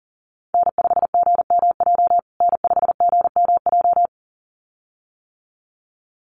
Morse code, heard as a keyed tone, about 40 wpm, 700 Hz. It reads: N5GMJ N5GMJ
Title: USA Call Signs Repeated 2x 40wpm